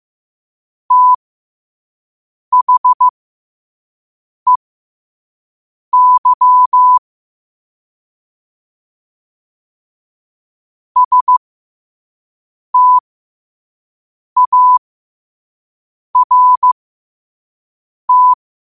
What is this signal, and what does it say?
Morse code, heard as a keyed tone, about 15 wpm, spaced as for 5 wpm, 1000 Hz. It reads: THEY START
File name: qotdfiles / 5wpm